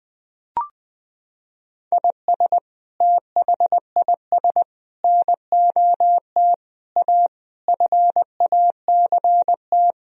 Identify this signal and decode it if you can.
Morse code, heard as a keyed tone, about 21 wpm, 700 Hz. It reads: IS THIS NOT A FACT
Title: Sentences from Top 1000 Words - Review Part 4 - 20wpm